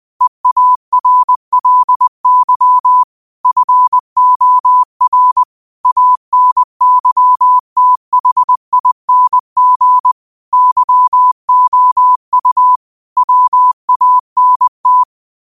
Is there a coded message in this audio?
Yes: Morse code from 0.2 to 15.0 s, heard as a keyed tone, about 20 wpm, 1000 Hz. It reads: EARLY FOR ANYTHING YOU WANT